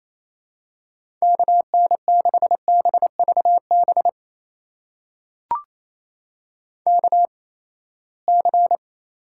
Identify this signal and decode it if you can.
Morse code, heard as a keyed tone, about 28 wpm, spaced as for 19 wpm, 700 Hz. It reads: KN6BVB KC
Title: USA Call Signs - 6 Characters - 8x Character Spacing - 28wpm